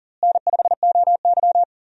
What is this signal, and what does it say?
Morse code, heard as a keyed tone, about 40 wpm, 700 Hz. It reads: N5OY
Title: USA Call Signs - 4 Characters - Repeated 2x - 40wpm